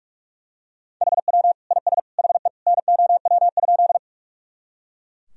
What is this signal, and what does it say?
Morse code, heard as a keyed tone, about 41 wpm, 700 Hz. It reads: HW IS HE NOW?
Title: Sentences from Top 100 Words and Abbreviations 45wpm